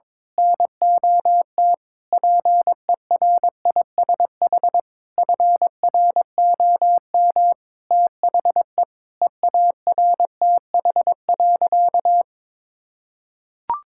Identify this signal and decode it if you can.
Morse code, heard as a keyed tone, about 22 wpm, 700 Hz. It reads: NOT PERISH FROM THE EARTH.
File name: Sentences with 9-Word Phrases - 22wpm